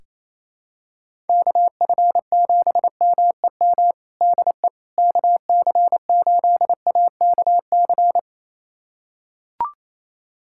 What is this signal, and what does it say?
Morse code, heard as a keyed tone, about 28 wpm, 700 Hz. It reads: KF7MEM DE KC8AKC
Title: USA Callsign DE Callsign - 6 Characters - 28wpm